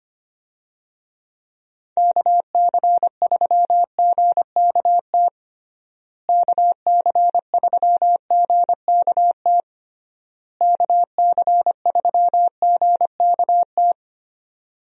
Morse code, heard as a keyed tone, about 25 wpm, 700 Hz. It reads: KC3GKT KC3GKT KC3GKT
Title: USA Call Signs - 6 Characters - Repeated 3x - 25wpm